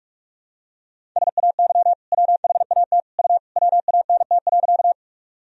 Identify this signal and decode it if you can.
Morse code, heard as a keyed tone, about 46 wpm, 700 Hz. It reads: SAY WHAT U WANT.
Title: Sentences from Top 200 Words and Abbreviations 45wpm